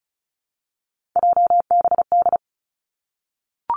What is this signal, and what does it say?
Morse code, heard as a keyed tone, about 36 wpm, 700 Hz. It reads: JBD
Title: Three Challenging Letter-Number Combinations 35wpm